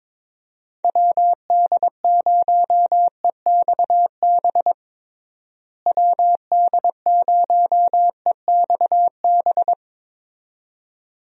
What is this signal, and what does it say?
Morse code, heard as a keyed tone, about 22 wpm, 700 Hz. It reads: WD0EXB WD0EXB
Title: USA Call Signs - 6 Characters - Repeated 2x - 22wpm